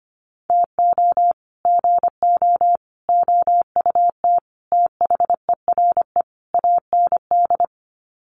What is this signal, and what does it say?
Morse code, heard as a keyed tone, about 25 wpm, 700 Hz. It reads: TO GO OUT THERE AND
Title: Sets of 5 Words Encore2 25wpm